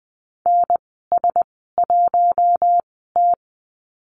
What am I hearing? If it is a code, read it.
Morse code, heard as a keyed tone, about 20 wpm, spaced as for 14 wpm, 700 Hz. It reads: NS1T
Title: 062 Lesson 11g - Letter h - Call Signs 20wpm